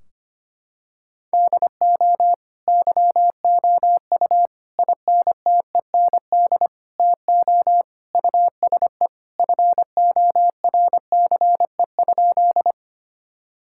Morse code, heard as a keyed tone, about 25 wpm, 700 Hz. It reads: DO YOU INTEND TO USE FORCE?
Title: Sentences from Top 1600 Words 25wpm